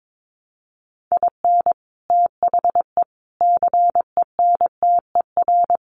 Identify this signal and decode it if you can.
Morse code, heard as a keyed tone, about 22 wpm, 700 Hz. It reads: IN THE CENTER
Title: Sets of 4 Words 22wpm